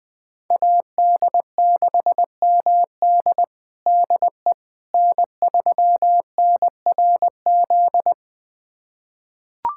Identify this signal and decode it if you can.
Morse code, heard as a keyed tone, about 21 wpm, 700 Hz. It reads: AD6MD DE N3NRZ
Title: USA Callsign DE Callsign - 5 Characters - 20wpm